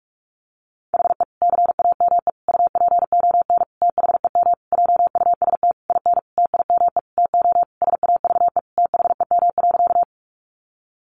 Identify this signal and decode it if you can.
Morse code, heard as a keyed tone, about 44 wpm, 700 Hz. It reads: HE CAME UPON THEM JUST IN TIME TO SAVE THEM.
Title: Sentences from Top 1100 Words 45wpm